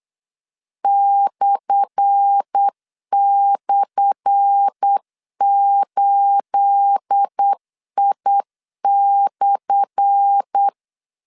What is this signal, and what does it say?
Morse code, heard as a keyed tone, about 8 wpm, 800 Hz. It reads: //8I/
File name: LEZIONI / 10 SIGNS / 10A SLASH 8wpm 3452